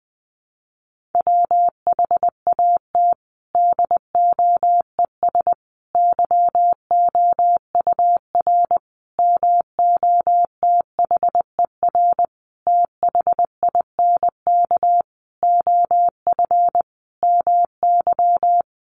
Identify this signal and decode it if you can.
Morse code, heard as a keyed tone, about 20 wpm, 700 Hz. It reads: WHAT DOES YOUR MOTHER THINK OF MY
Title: Sentences from Top 1400 Words 20wpm